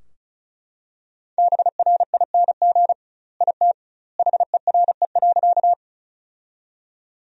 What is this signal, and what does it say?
Morse code, heard as a keyed tone, about 34 wpm, 700 Hz. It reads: BRING IT HERE.
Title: 2x Word-Spacing - Sentences from Top 500 Words - 35wpm